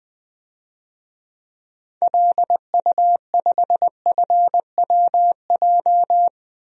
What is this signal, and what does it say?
Morse code, heard as a keyed tone, about 20 wpm, 700 Hz. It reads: LU5FWJ
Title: DXCC-100 Call Signs - 6 Characters - 20wpm